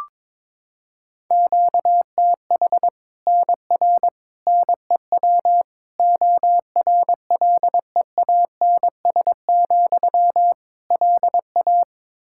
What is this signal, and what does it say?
Morse code, heard as a keyed tone, about 22 wpm, 700 Hz. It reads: QTH NR NEW ORLEANS, LA
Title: Practice QSO - Exchange 1 - Beginning - 22wpm